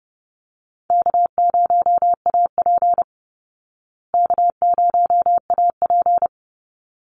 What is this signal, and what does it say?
Morse code, heard as a keyed tone, about 30 wpm, 700 Hz. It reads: K0AP K0AP
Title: Easy Call Signs Repeated 2x - 30wpm